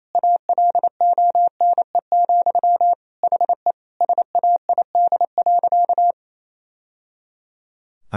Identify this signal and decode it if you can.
Morse code, heard as a keyed tone, about 28 wpm, 690 Hz. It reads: ALONE, HE SAID.
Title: Sentences from Top 1000 Words - Review Part 2 - 28wpm